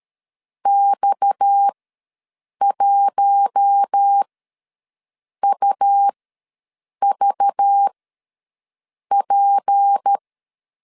Morse code, heard as a keyed tone, about 13 wpm, spaced as for 6 wpm, 800 Hz. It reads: X1UVP
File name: LEZIONI / 6 PKX0 / 6 PKX0 ws 12wpm 3448